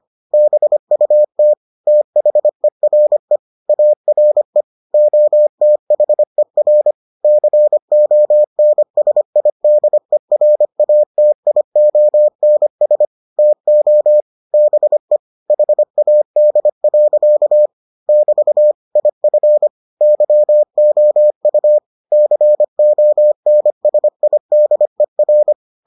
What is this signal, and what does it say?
Morse code, heard as a keyed tone, about 25 wpm, 600 Hz. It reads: BUT THERE ARE OTHER CONSIDERATIONS TO BE HAD. <BT> IF YOU CONSIDER